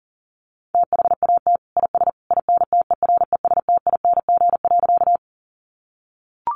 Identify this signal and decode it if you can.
Morse code, heard as a keyed tone, about 40 wpm, 700 Hz. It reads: THAT IS INTERESTING.
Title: Sentences from Top 1000 Words 40wpm